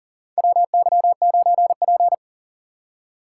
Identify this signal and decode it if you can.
Morse code, heard as a keyed tone, about 40 wpm, 700 Hz. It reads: WY9P
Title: USA Call Signs - 4 Characters - 6x Character Spacing - 40wpm